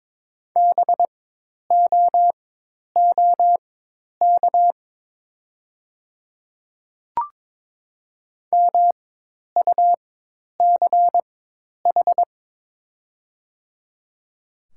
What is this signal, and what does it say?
Morse code, heard as a keyed tone, about 22 wpm, spaced as for 10 wpm, 700 Hz. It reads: BOOK MUCH
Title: 4x Character Spacing - Top 300 Words - 22wpm